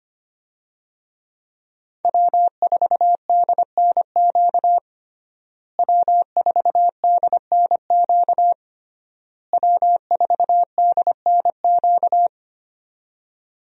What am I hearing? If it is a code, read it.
Morse code, heard as a keyed tone, about 25 wpm, 700 Hz. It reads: W4DNQ W4DNQ W4DNQ
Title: USA Call Signs - 5 Characters - Repeated 3x - 25wpm